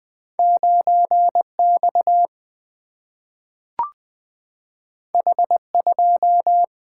Morse code, heard as a keyed tone, about 20 wpm, 700 Hz. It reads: 9 X H 2
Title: Single-Letter Single-Number 20wpm